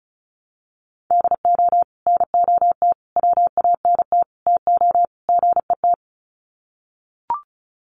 Morse code, heard as a keyed tone, about 34 wpm, 700 Hz. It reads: DO NOT WANT TO GET
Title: Sets of 5 Words 35wpm